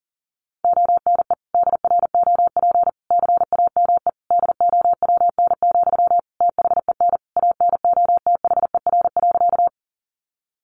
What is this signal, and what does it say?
Morse code, heard as a keyed tone, about 40 wpm, 700 Hz. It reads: ONE DROP CAME DOWN, THEN ANOTHER.
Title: Sentences from Top 1400 Words 40wpm